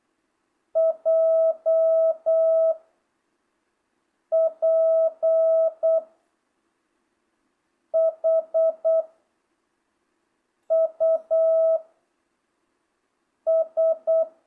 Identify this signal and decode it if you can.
Morse code, heard as a keyed tone, about 8 wpm, 660 Hz. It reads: J P H U S